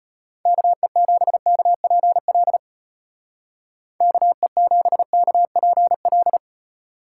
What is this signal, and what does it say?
Morse code, heard as a keyed tone, about 35 wpm, 700 Hz. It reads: KE7KPL KE7KPL